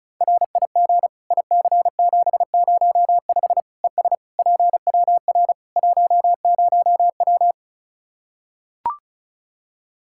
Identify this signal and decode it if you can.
Morse code, heard as a keyed tone, about 36 wpm, 700 Hz. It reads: RIG IC705 ES PWR 10W
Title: Practice QSO - Exchange 2 - Info - 35wpm